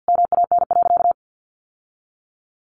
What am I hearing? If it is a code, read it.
Morse code, heard as a keyed tone, about 50 wpm, 700 Hz. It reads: MAN.
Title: Sentences from Top 1100 Words 50wpm